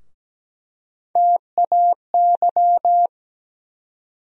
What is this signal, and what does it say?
Morse code, heard as a keyed tone, about 17 wpm, 700 Hz. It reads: TAY
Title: USA Call Sign Postfixes - 3 Letters 17wpm